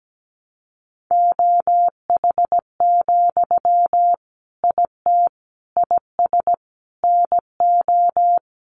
Morse code, heard as a keyed tone, about 17 wpm, 700 Hz. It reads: OH, IT IS NO
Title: Sentences from Top 1200 Words 17wpm